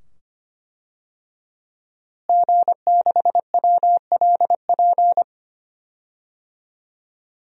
Morse code, heard as a keyed tone, about 25 wpm, 700 Hz. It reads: G6WLP
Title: DXCC-100 Call Signs - 5 Characters - Repeated 2x - 25wpm